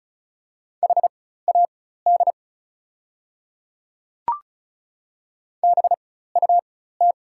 Morse code, heard as a keyed tone, about 36 wpm, spaced as for 16 wpm, 700 Hz. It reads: HAD BUT
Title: 4x Character Spacing - Top 100 Words - 35wpm